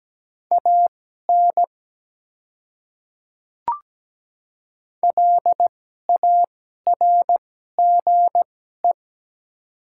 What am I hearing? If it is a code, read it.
Morse code, heard as a keyed tone, about 17 wpm, spaced as for 12 wpm, 700 Hz. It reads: AN LARGE
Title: 2x Character Spacing - Top 200 Words - 17wpm